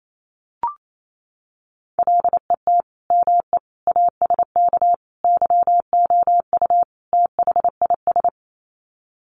Morse code, heard as a keyed tone, about 28 wpm, 700 Hz. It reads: LET ME ASK YOU THIS
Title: Sets of 5 Words 28wpm